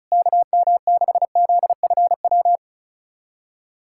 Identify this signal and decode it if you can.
Morse code, heard as a keyed tone, about 36 wpm, 700 Hz. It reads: KM6ZFW
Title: SOTA QSO Practice 35wpm